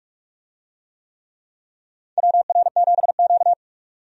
Morse code, heard as a keyed tone, about 46 wpm, 700 Hz. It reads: WR7Q
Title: USA Call Signs - 4 Characters - 45wpm